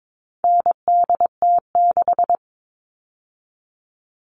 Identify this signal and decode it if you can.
Morse code, heard as a keyed tone, about 22 wpm, 700 Hz. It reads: NDT6
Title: All Cut Numbers - 4-Digit 22wpm